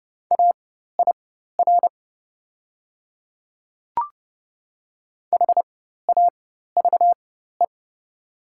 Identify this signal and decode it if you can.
Morse code, heard as a keyed tone, about 30 wpm, spaced as for 14 wpm, 700 Hz. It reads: AIR HAVE
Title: 4x Character Spacing - Top 300 Words - 30wpm